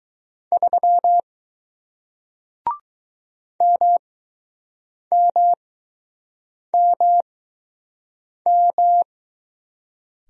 Morse code, heard as a keyed tone, about 20 wpm, spaced as for 6 wpm, 700 Hz. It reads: 3 MMMM